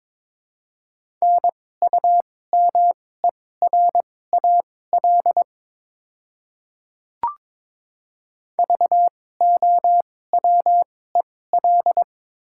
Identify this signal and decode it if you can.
Morse code, heard as a keyed tone, about 22 wpm, spaced as for 16 wpm, 690 Hz. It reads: NUMERAL VOWEL
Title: 2x Character Spacing - Top 500 Words - 22wpm